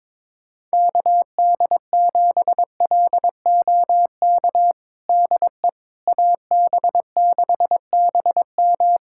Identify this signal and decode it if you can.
Morse code, heard as a keyed tone, about 22 wpm, 700 Hz. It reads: KD7LOK DE AB6BM
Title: USA Callsign DE Callsign 22wpm